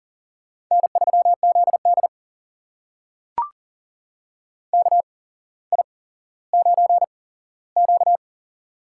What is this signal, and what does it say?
Morse code, heard as a keyed tone, about 40 wpm, spaced as for 17 wpm, 700 Hz. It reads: N3ZD KI9Q
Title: USA Call Signs - 4 Characters - 8x Character Spacing - 40wpm